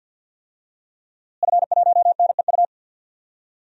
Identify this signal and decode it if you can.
Morse code, heard as a keyed tone, about 52 wpm, 700 Hz. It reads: F1NEU
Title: DXCC-100 Call Signs - 5 Characters - Repeated 2x - 50wpm